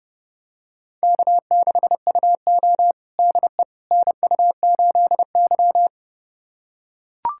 Morse code, heard as a keyed tone, about 30 wpm, 700 Hz. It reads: K6UO DE NU8Y
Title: USA Callsign DE Callsign - 4 Characters - 30wpm